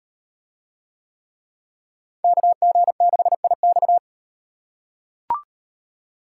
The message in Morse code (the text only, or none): KG6IX